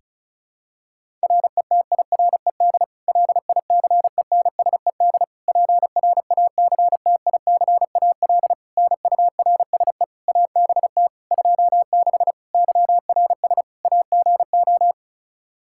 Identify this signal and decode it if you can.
Morse code, heard as a keyed tone, about 34 wpm, 700 Hz. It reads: RETIRED LICENSED PRACTICAL NURSE ABT 26 YRS AGO
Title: Practice QSO - Exchange 3 - More Info - 35wpm